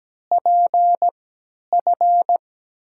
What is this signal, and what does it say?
Morse code, heard as a keyed tone, about 17 wpm, spaced as for 10 wpm, 700 Hz. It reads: PF
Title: USA Call Signs - 6 Characters - 3x Character Spacing - 17wpm